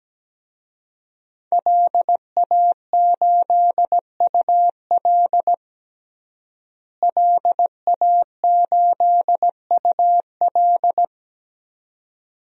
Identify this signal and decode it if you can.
Morse code, heard as a keyed tone, about 17 wpm, 700 Hz. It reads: LA8UL LA8UL
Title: DXCC-100 Call Signs - 5 Characters - Repeated 2x - 17wpm